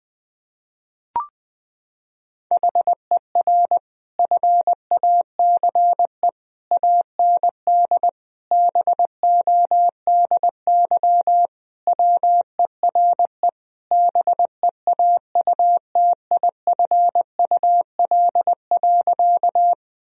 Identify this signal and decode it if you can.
Morse code, heard as a keyed tone, about 20 wpm, 700 Hz. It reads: HER FACE AND BODY WERE BEAUTIFUL.
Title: Sentences from Top 1000 Words - Review Part 2 - 20wpm